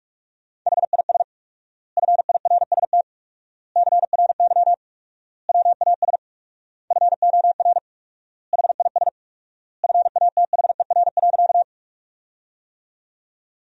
Morse code, heard as a keyed tone, about 47 wpm, 700 Hz. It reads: HIS FIRST CRY WAS FOR HIS FATHER.
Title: 4x Word-Spacing - Sentences from Top 500 Words - 45wpm